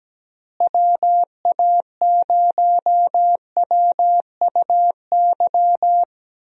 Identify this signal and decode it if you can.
Morse code, heard as a keyed tone, about 17 wpm, 700 Hz. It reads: WA0WUY
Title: USA Call Signs - 6 Characters - Repeated 3x - 17wpm